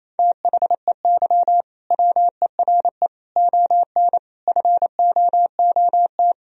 Morse code, heard as a keyed tone, about 27 wpm, 690 Hz. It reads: THEY WERE ON FOOT
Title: Sentences from Top 1000 Words - Review Part 3 - 28wpm